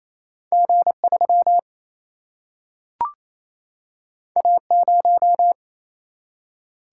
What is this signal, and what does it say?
Morse code, heard as a keyed tone, about 28 wpm, 700 Hz. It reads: G 3 A 0
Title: Single-Letter Single-Number 28wpm